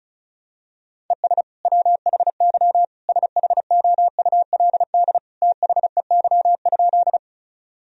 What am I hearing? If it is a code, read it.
Morse code, heard as a keyed tone, about 36 wpm, 700 Hz. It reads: ES WHY SHOULD THEY?
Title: Sentences from Top 300 Words and Abbreviations 35wpm